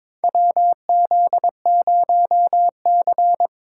Morse code, heard as a keyed tone, about 21 wpm, 700 Hz. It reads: WZ0C
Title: USA Call Signs - 2x Character Spacing - 22wpm